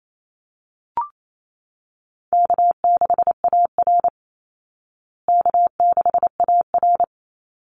Morse code, heard as a keyed tone, about 28 wpm, 700 Hz. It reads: K6AR K6AR
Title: Easy Call Signs Repeated 2x - 28wpm